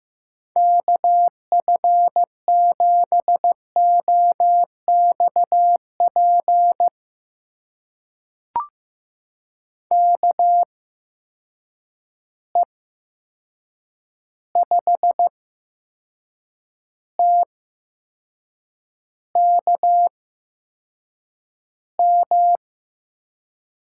Morse code, heard as a keyed tone, about 15 wpm, spaced as for 6 wpm, 700 Hz. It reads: KF7OXP KE5TKM
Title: USA Call Signs - 6 Characters - 8x Character Spacing - 15wpm